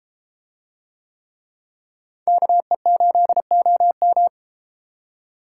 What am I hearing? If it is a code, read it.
Morse code, heard as a keyed tone, about 33 wpm, 700 Hz. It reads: KE8OM